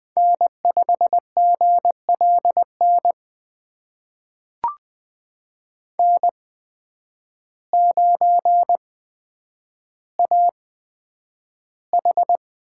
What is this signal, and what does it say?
Morse code, heard as a keyed tone, about 20 wpm, spaced as for 9 wpm, 700 Hz. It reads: N5GLN N9AH
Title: USA Call Signs - 8x Character Spacing - 20wpm